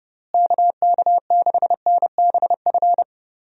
Morse code, heard as a keyed tone, about 30 wpm, 700 Hz. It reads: KK6NBF
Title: USA Call Signs - 6 Characters - Repeated 2x - 30wpm